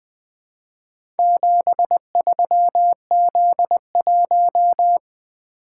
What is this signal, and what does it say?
Morse code, heard as a keyed tone, about 20 wpm, 700 Hz. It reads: 73Z1
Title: Four Letter-Number Combinations - 20wpm